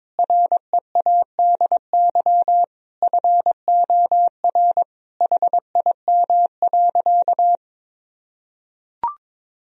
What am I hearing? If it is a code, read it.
Morse code, heard as a keyed tone, about 21 wpm, 700 Hz. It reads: READY FOR HIM.
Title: Sentences from Top 1000 Words - Review Part 3 - 22wpm